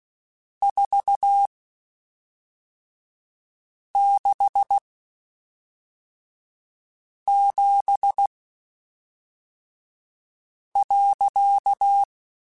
Morse code, heard as a keyed tone, about 16 wpm, spaced as for 3 wpm, 800 Hz. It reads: 467.
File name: (spaced) / QSOs Spaced-mp3s / kc7haa @5 sp